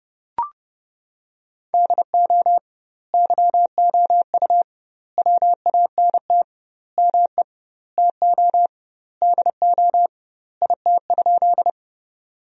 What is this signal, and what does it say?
Morse code, heard as a keyed tone, about 30 wpm, 700 Hz. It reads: DO YOU WANT ME TO DO IT?
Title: 2x Word-Spacing - Sentences from Top 200 Words - 30wpm